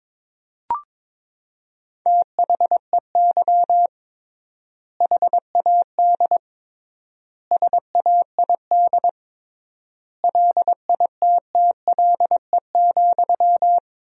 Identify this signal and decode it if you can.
Morse code, heard as a keyed tone, about 22 wpm, 700 Hz. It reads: THEY HAD SAID LITTLE,
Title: 3x Word-Spacing - Sentences from Top 400 Words - 22wpm